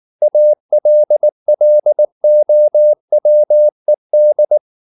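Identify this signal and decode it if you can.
Morse code, heard as a keyed tone, about 19 wpm, 600 Hz. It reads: ALLOWED
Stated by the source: Morse code at 19 wpm from CW med Gnister, reading Ken Follett's novel Never in English.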